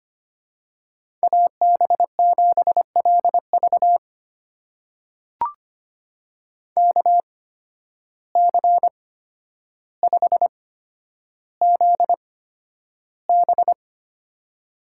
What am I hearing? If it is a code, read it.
Morse code, heard as a keyed tone, about 25 wpm, spaced as for 11 wpm, 700 Hz. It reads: AB7LV KC5ZB
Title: USA Call Signs - 5 Characters - 8x Character Spacing - 25wpm